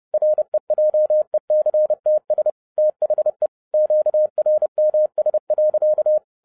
Morse code, heard as a keyed tone, about 30 wpm, 610 Hz. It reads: REJECTS THE QRMS.
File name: 2012-01-mp3 / n11-I DIDNT MISS THE SOLAR CYCLE 24-30wpm